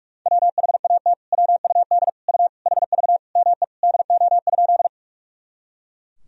Morse code, heard as a keyed tone, about 41 wpm, 700 Hz. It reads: WHAT WUD U HV ME DO?
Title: Sentences from Top 200 Words and Abbreviations 45wpm